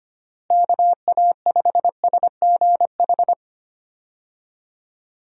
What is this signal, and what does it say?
Morse code, heard as a keyed tone, about 25 wpm, 700 Hz. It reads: KA5SGH
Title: USA Call Signs - 6 Characters - Repeated 2x - 25wpm